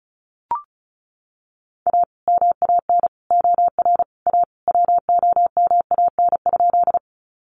Morse code, heard as a keyed tone, about 34 wpm, 700 Hz. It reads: A MAN OR A WOMAN?
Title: Sentences from Top 1100 Words 35wpm